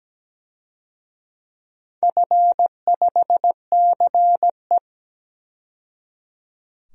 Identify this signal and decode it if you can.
Morse code, heard as a keyed tone, about 17 wpm, 700 Hz. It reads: F5CE
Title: DXCC-100 Call Signs - 4 Characters - 17wpm